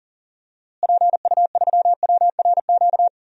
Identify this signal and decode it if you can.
Morse code, heard as a keyed tone, about 40 wpm, 700 Hz. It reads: PU3WRQ
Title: DXCC-100 Call Signs - 6 Characters - Repeated 2x - 40wpm